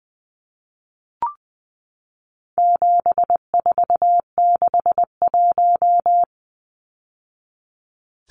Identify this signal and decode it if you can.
Morse code, heard as a keyed tone, about 20 wpm, 700 Hz. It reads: 7461
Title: Numbers - 4-Digit 20wpm